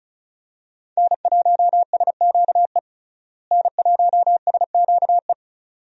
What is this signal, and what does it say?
Morse code, heard as a keyed tone, about 36 wpm, 700 Hz. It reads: N1SQE N1SQE
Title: USA Call Signs - 5 Characters - Repeated 2x - 35wpm